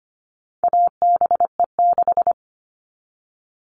Morse code, heard as a keyed tone, about 25 wpm, 700 Hz. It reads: ABE6
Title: All Cut Numbers - 4-Digit 25wpm